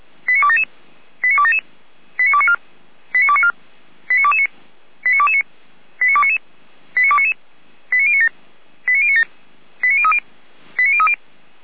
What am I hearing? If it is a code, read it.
Five-tone selective call (ZVEI), heard as quick runs of five tones, tones of about 70 ms. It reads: 89277 89277 89294 89294 89229 89229 89299 89299 89097 89097 89030 89030
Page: /455/SDR/ZVEI/